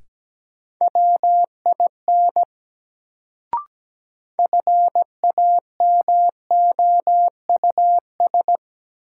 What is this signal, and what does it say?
Morse code, heard as a keyed tone, about 17 wpm, 700 Hz. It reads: WIN FAMOUS
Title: Sentences from Top 1500 Words - New Word Drill - 17wpm